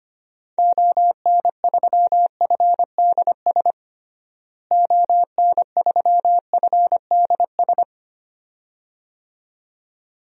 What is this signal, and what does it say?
Morse code, heard as a keyed tone, about 25 wpm, 700 Hz. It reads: ON3FDS ON3FDS
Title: DXCC-100 Call Signs - 6 Characters - Repeated 2x - 25wpm